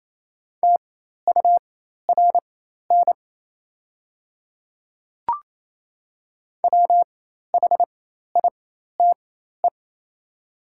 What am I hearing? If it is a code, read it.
Morse code, heard as a keyed tone, about 28 wpm, spaced as for 13 wpm, 700 Hz. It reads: TURN WHITE
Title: 4x Character Spacing - Top 500 Words - 28wpm